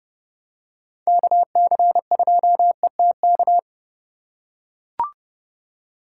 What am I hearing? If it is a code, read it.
Morse code, heard as a keyed tone, about 29 wpm, 700 Hz. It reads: KC2ETK